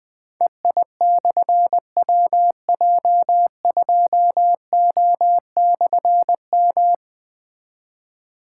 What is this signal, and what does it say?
Morse code, heard as a keyed tone, about 20 wpm, 700 Hz. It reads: EI/WJ2O/M
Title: Difficult Call Signs 20wpm